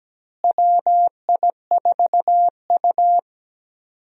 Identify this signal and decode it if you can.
Morse code, heard as a keyed tone, about 17 wpm, 700 Hz. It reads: WI4U
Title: USA Call Signs - 4 Characters - Repeated 3x - 17wpm